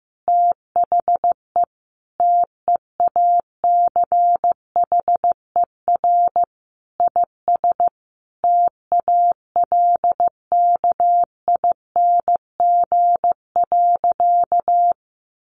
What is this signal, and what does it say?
Morse code, heard as a keyed tone, about 15 wpm, 700 Hz. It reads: THE TEACHER IS TALKING.
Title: Additional Easy Phrases 15wpm